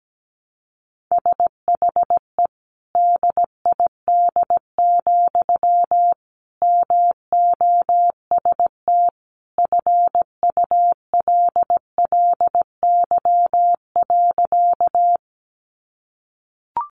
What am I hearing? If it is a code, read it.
Morse code, heard as a keyed tone, about 17 wpm, 700 Hz. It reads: SHE DID, MOST FULLY.
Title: Sentences from Top 1300 Words 17wpm